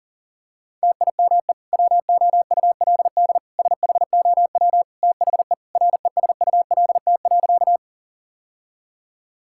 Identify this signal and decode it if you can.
Morse code, heard as a keyed tone, about 37 wpm, 700 Hz. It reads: TIME WOULD SHOW THE RESULT.
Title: Sentences from Top 1000 Words - Review Part 5 - 40wpm